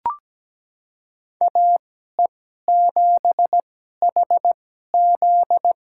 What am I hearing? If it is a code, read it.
Morse code, heard as a keyed tone, about 17 wpm, spaced as for 12 wpm, 700 Hz. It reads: AE7HZ
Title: USA Call Signs - 5 Characters - 2x Character Spacing - 17wpm